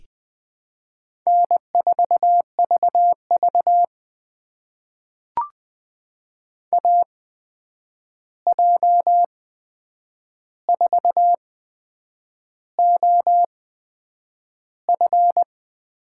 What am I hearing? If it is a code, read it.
Morse code, heard as a keyed tone, about 20 wpm, spaced as for 8 wpm, 700 Hz. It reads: N4VV AJ4OF
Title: USA Call Signs - 8x Character Spacing - 20wpm